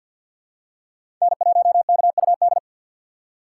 Morse code, heard as a keyed tone, about 53 wpm, 700 Hz. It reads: N1KUD
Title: USA Call Signs - 5 Characters - 6x Character Spacing - 50wpm